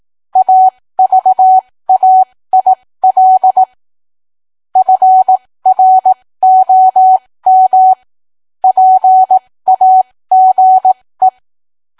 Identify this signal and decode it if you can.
Morse code, heard as a keyed tone, about 18 wpm, spaced as for 15 wpm, 750 Hz. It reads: AVAIL FROM PAGE